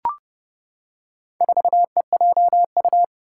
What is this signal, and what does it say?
Morse code, heard as a keyed tone, about 30 wpm, 700 Hz. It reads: 4EJU
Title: Four Challenging Letter-Number Combinations 30wpm